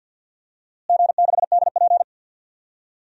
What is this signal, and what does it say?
Morse code, heard as a keyed tone, about 52 wpm, 700 Hz. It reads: G6DP
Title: DXCC-100 Call Signs - 4 Characters - Repeated 3x - 50wpm